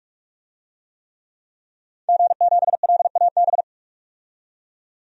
Morse code, heard as a keyed tone, about 46 wpm, 700 Hz. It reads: G7LAB
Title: DXCC-100 Call Signs - 5 Characters - 45wpm